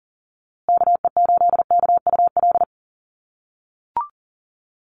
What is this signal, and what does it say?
Morse code, heard as a keyed tone, about 40 wpm, 700 Hz. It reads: KE8KUL
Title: USA Call Signs Repeated 3x 40wpm